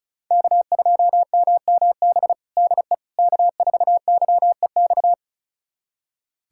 Morse code, heard as a keyed tone, about 34 wpm, 700 Hz. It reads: K2MMB DE K4YEX
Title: USA Callsign DE Callsign - 5 Characters - 35wpm